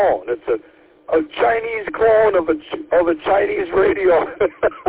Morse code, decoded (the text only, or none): none